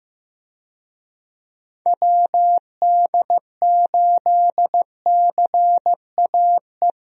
Morse code, heard as a keyed tone, about 15 wpm, 700 Hz. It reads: WD8CAE
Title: USA Call Signs - 6 Characters - Repeated 3x - 15wpm